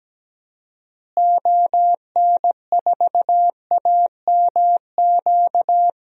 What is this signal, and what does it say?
Morse code, heard as a keyed tone, about 17 wpm, 700 Hz. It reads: ON4AMQ
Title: DXCC-100 Call Signs - 6 Characters - Repeated 2x - 17wpm